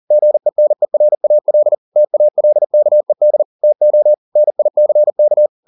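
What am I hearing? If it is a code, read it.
Morse code, heard as a keyed tone, about 40 wpm, 600 Hz. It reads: GENERAL TALKED TO NICK
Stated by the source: Morse code at 40 wpm, practice text, CW med Gnister.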